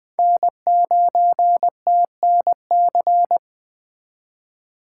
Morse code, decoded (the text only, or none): N9TNC